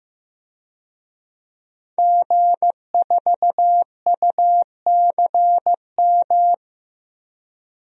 Morse code, heard as a keyed tone, about 15 wpm, 700 Hz. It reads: G4UCM
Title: DXCC-100 Call Signs - 5 Characters - Repeated 2x - 15wpm